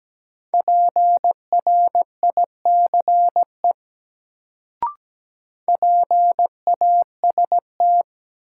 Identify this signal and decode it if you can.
Morse code, heard as a keyed tone, about 17 wpm, 700 Hz. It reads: PRICE PAST
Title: Sentences from Top 600 Words - New Word Drill - 17wpm